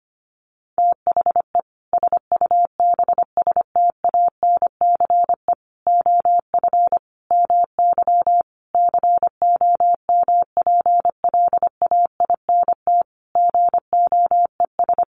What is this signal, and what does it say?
Morse code, heard as a keyed tone, about 25 wpm, 700 Hz. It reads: THE SUBSTANCE OF MY COMPLAINT GOES
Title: Sentences from Top 1500 Words 25wpm